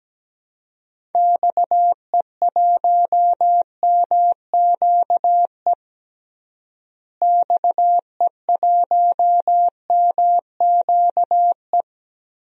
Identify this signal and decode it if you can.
Morse code, heard as a keyed tone, about 17 wpm, 700 Hz. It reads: XE1MQE XE1MQE
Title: DXCC-100 Call Signs - 6 Characters - Repeated 3x - 17wpm